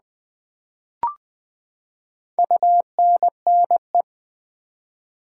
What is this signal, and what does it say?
Morse code, heard as a keyed tone, about 20 wpm, 700 Hz. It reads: UNNE
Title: Build A Word - 4 Letters At A Time - 11-Letter Words - 20wpm